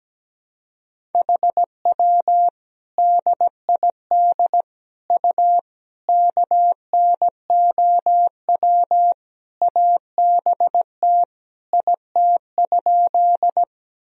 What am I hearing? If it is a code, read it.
Morse code, heard as a keyed tone, about 17 wpm, 700 Hz. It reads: HW DID U KNOW ABT IT?
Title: Sentences from Top 100 Words and Abbreviations 17wpm